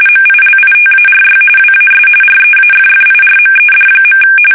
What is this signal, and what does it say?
RTTY, about 45.45 baud. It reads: MILITARY WAS REAFFIRMED.
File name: fsk850 / usnavy5